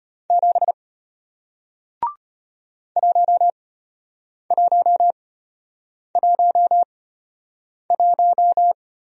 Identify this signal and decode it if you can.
Morse code, heard as a keyed tone, about 33 wpm, spaced as for 8 wpm, 700 Hz. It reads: 7 1111